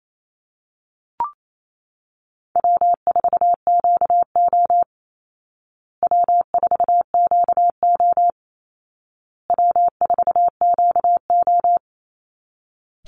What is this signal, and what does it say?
Morse code, heard as a keyed tone, about 28 wpm, 700 Hz. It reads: W4QO W4QO W4QO
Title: Easy Call Signs Repeated 3x - 28wpm